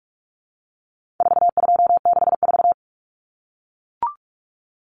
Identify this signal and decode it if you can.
Morse code, heard as a keyed tone, about 44 wpm, 700 Hz. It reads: 4264
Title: Numbers - 4-Digit 45wpm